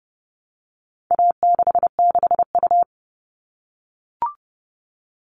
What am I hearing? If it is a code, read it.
Morse code, heard as a keyed tone, about 30 wpm, 700 Hz. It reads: A66U
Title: All Cut Numbers - 4-Digit 30wpm